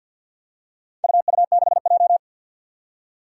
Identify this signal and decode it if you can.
Morse code, heard as a keyed tone, about 52 wpm, 700 Hz. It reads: UU6J
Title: Four Challenging Letter-Number Combinations 50wpm